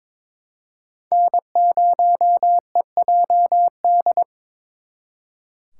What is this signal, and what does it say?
Morse code, heard as a keyed tone, about 26 wpm, 700 Hz. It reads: N0EJD
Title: Speed Racing - USA 1x3 Call Signs 22wpm